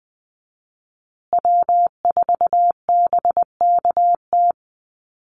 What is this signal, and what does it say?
Morse code, heard as a keyed tone, about 20 wpm, 700 Hz. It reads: W4BKT
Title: USA Call Signs Repeated 2x 20wpm